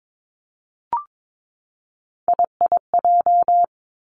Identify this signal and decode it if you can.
Morse code, heard as a keyed tone, about 22 wpm, 700 Hz. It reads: IIJ
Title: Three Challenging Letter-Number Combinations 22wpm